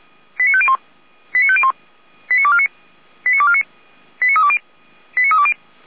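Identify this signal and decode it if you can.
Five-tone selective call (ZVEI), heard as quick runs of five tones, tones of about 70 ms. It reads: 89501 89501 89249 89249 89320 89320
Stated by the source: ZVEI five-tone selective call received on an SDR.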